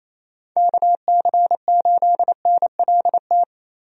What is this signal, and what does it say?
Morse code, heard as a keyed tone, about 28 wpm, 700 Hz. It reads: KC8NLT
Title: USA Call Signs - 6 Characters - 28wpm